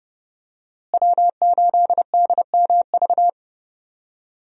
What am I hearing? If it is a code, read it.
Morse code, heard as a keyed tone, about 31 wpm, 700 Hz. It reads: W8DMV
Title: USA Call Signs - 5 Characters - Repeated 2x - 30wpm